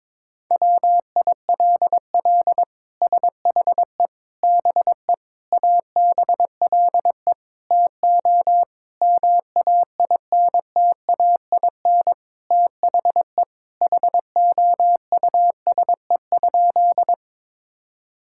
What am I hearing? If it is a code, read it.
Morse code, heard as a keyed tone, about 22 wpm, 700 Hz. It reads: WILL SHE BE ABLE TO MAINTAIN THE HOUSE?
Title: Sentences from Top 1000 Words - Review Part 5 - 22wpm